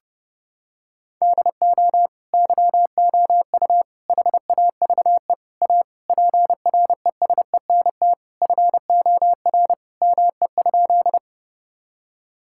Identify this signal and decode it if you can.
Morse code, heard as a keyed tone, about 30 wpm, 700 Hz. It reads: DO YOU HAVE A PRESENT FOR ME?
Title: Sentences from Top 1000 Words - Review Part 5 - 30wpm